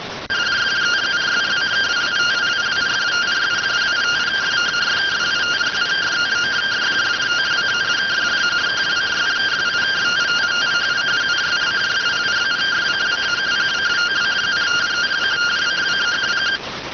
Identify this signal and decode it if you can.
Two-tone teleprinter signal, about 170 Hz shift.